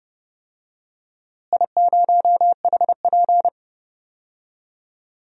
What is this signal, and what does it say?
Morse code, heard as a keyed tone, about 28 wpm, 700 Hz. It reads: I0HP
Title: DXCC-100 Call Signs - 4 Characters - Repeated 3x - 30wpm